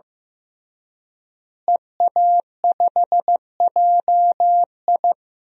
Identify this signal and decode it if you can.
Morse code, heard as a keyed tone, about 16 wpm, 700 Hz. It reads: EA5JI
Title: DXCC-100 Call Signs - 5 Characters - Repeated 3x - 15wpm